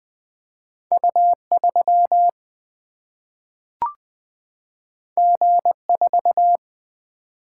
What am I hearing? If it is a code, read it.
Morse code, heard as a keyed tone, about 20 wpm, 700 Hz. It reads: U 3 G 4
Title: Single-Letter Single-Number 20wpm